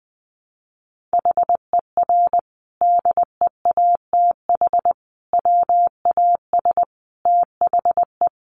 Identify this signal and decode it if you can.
Morse code, heard as a keyed tone, about 20 wpm, 700 Hz. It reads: HER DEATH WAS THE
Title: Sentences from Top 1300 Words 20wpm